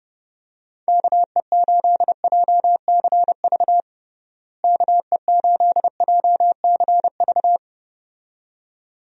Morse code, heard as a keyed tone, about 30 wpm, 700 Hz. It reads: KE8JCV KE8JCV
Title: USA Call Signs - 6 Characters - Repeated 2x - 30wpm